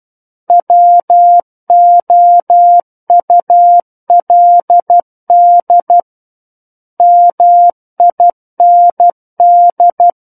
Morse code, heard as a keyed tone, about 12 wpm, 700 Hz. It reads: WOULD MIND
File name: Book One / 03-Chapter 3 - 12 WPM